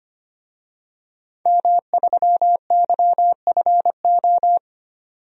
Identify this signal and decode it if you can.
Morse code, heard as a keyed tone, about 25 wpm, 700 Hz. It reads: M3YFO
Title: DXCC-100 Call Signs - 5 Characters - 25wpm